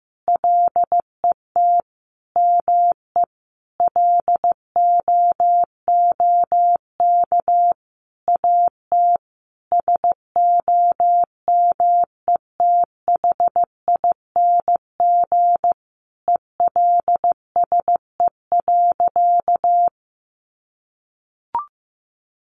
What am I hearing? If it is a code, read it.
Morse code, heard as a keyed tone, about 15 wpm, 700 Hz. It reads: LET ME LOOK AT SOMETHING ELSE.
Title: Sentences from Top 800 Words 15wpm